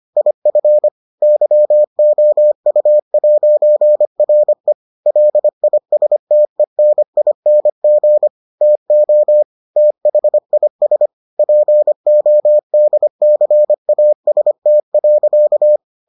Morse code, heard as a keyed tone, about 25 wpm, 600 Hz. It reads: IF YOU'RE LISTENING TO THIS PODCAST.